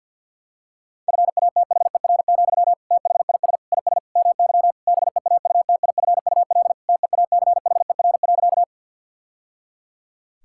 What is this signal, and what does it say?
Morse code, heard as a keyed tone, about 50 wpm, 700 Hz. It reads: FATHER, THIS IS MY BEAUTIFUL TEACHER.
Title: Sentences from Top 1000 Words - Review Part 6 - 50wpm